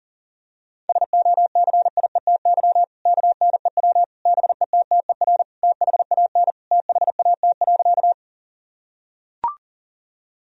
Morse code, heard as a keyed tone, about 40 wpm, 700 Hz. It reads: SOCIETY KNEW BETTER THAN THAT.
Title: Sentences from Top 1000 Words - Review Part 4 - 40wpm